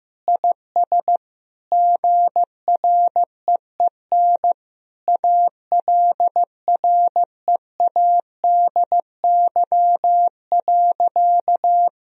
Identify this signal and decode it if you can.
Morse code, heard as a keyed tone, about 16 wpm, 700 Hz. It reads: IS GREEN ALREADY.
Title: Sentences from Top 1000 Words - Review Part 2 - 15wpm